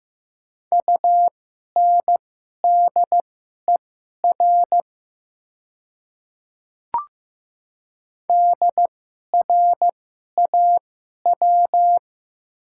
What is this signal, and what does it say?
Morse code, heard as a keyed tone, about 15 wpm, spaced as for 11 wpm, 700 Hz. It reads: UNDER DRAW
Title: 2x Character Spacing - Top 500 Words - 15wpm